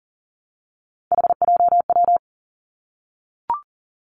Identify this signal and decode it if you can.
Morse code, heard as a keyed tone, about 38 wpm, 700 Hz. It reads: HJW
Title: Three Challenging Letter-Number Combinations 40wpm